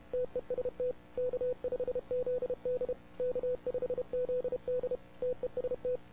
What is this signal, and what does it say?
Morse code, heard as a keyed tone, about 31 wpm, 500 Hz. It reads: TEST K5ZD K5ZD TEST
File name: wwcw08 / k5zd wwcw08 0200z